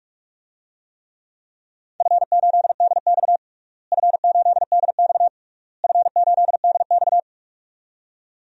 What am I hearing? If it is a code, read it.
Morse code, heard as a keyed tone, about 46 wpm, 700 Hz. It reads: F8DX F8DX F8DX
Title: DXCC-100 Call Signs - 4 Characters - Repeated 3x - 45wpm